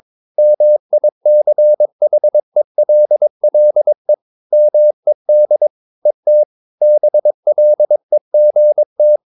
Morse code, heard as a keyed tone, about 22 wpm, 600 Hz. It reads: MICHELLE MED ET BLEGT
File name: Ferie QRS 22wpm